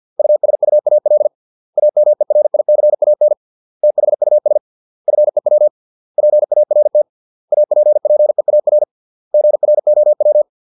Morse code, heard as a keyed tone, about 50 wpm, 600 Hz. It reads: USUAL AMERICAN THUS FEW PART APPEAR GROW
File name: 632 words / 632 Words 50WPM